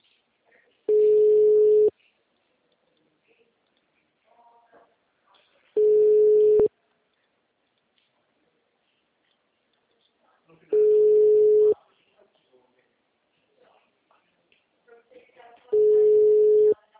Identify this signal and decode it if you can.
Ringback tone (Europe, 425 Hz), with speech.